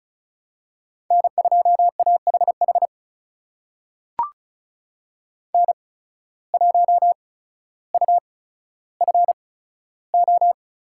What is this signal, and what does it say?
Morse code, heard as a keyed tone, about 34 wpm, spaced as for 15 wpm, 700 Hz. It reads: N2AHH N1UFO
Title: USA Call Signs - 5 Characters - 8x Character Spacing - 35wpm